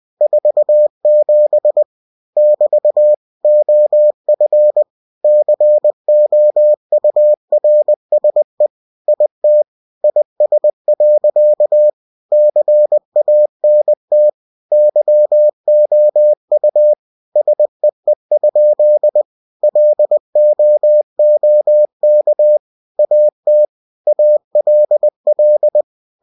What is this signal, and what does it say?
Morse code, heard as a keyed tone, about 20 wpm, 600 Hz. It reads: 47 <BT> OF COURSE IT IS. CANT YOU SEE? LOOK AT ALL
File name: cw-books / winnie-the-pooh / winnie-20wpm-page-0047